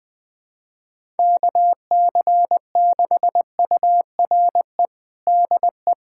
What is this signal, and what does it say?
Morse code, heard as a keyed tone, about 20 wpm, 700 Hz. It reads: KC6URE DE
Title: USA Callsign DE Callsign - 6 Characters - 20wpm